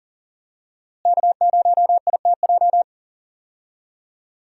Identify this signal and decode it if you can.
Morse code, heard as a keyed tone, about 37 wpm, 700 Hz. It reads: K0ITJ
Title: USA Call Signs - 5 Characters - Repeated 3x - 40wpm